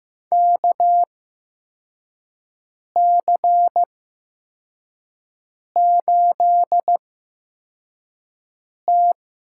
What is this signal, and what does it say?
Morse code, heard as a keyed tone, about 15 wpm, spaced as for 4 wpm, 700 Hz. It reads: KC8T
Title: USA Call Signs - 6 Characters - 8x Character Spacing - 15wpm